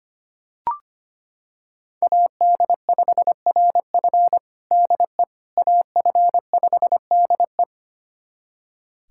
Morse code, heard as a keyed tone, about 25 wpm, 700 Hz. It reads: AD5RF DE AF5DE
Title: USA Callsign DE Callsign - 5 Characters - 25wpm